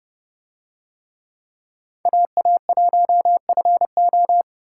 Morse code, heard as a keyed tone, about 30 wpm, 700 Hz. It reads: AA1FO